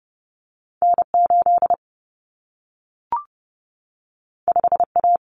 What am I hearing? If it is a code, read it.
Morse code, heard as a keyed tone, about 30 wpm, 700 Hz. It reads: N8 5A
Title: Common Cut Numbers - 2-Digit 30wpm